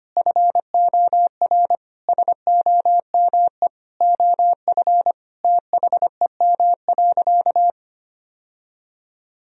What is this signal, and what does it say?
Morse code, heard as a keyed tone, about 25 wpm, 700 Hz. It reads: FOR SOME OF THEM.
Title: Sentences from Top 1000 Words - Review Part 3 - 25wpm